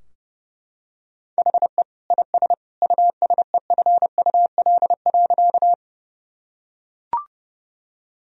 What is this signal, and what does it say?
Morse code, heard as a keyed tone, about 32 wpm, 700 Hz. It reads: HE IS USEFUL.
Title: Sentences from Top 1000 Words - Review Part 4 - 30wpm